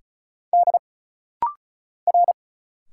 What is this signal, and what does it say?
Morse code, heard as a keyed tone, about 35 wpm, 700 Hz. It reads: D R
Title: Single Letters - Rapid-Fire 35wpm